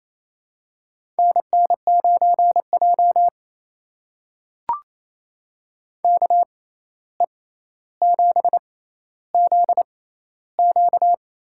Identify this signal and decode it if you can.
Morse code, heard as a keyed tone, about 28 wpm, spaced as for 13 wpm, 700 Hz. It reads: NN9J KE7ZQ
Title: USA Call Signs - 6x Character Spacing - 28wpm